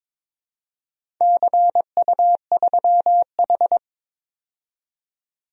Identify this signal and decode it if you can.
Morse code, heard as a keyed tone, about 22 wpm, 700 Hz. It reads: CU3H
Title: DXCC-100 Call Signs - 4 Characters - 22wpm